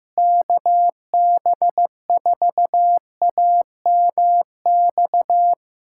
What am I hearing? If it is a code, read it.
Morse code, heard as a keyed tone, about 15 wpm, 700 Hz. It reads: KB4AMX
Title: USA Callsign DE Callsign - 6 Characters - 15wpm